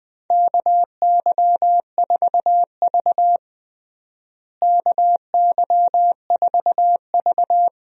Morse code, heard as a keyed tone, about 19 wpm, 700 Hz. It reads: KY4V KY4V
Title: USA Call Signs - 4 Characters - Repeated 2x - 20wpm